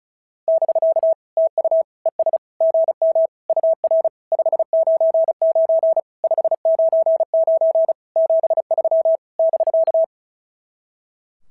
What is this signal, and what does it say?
Morse code, heard as a keyed tone, about 36 wpm, 690 Hz. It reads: <BK> TU ES GM UR 599 599 73 <BK>
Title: SOTA QSO Practice 35wpm